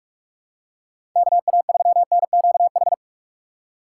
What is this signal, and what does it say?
Morse code, heard as a keyed tone, about 41 wpm, 700 Hz. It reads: KA3NQH
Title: USA Call Signs - 6 Characters - 3x Character Spacing - 45wpm